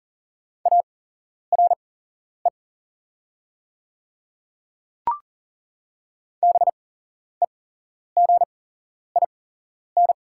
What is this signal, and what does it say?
Morse code, heard as a keyed tone, about 38 wpm, spaced as for 11 wpm, 700 Hz. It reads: ARE BEGIN
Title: 8x Character Spacing - Top 300 Words - 40wpm